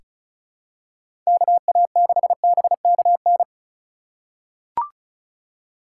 Morse code, heard as a keyed tone, about 36 wpm, 690 Hz. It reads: KA6BKN
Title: USA Call Signs - 6 Characters - 4x Character Spacing - 35wpm